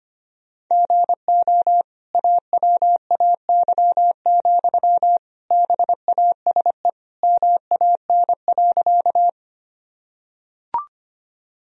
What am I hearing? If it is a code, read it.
Morse code, heard as a keyed tone, about 25 wpm, 700 Hz. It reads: GO AWAY, BASE MAN.
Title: Sentences from Top 1000 Words - Review Part 1 - 25wpm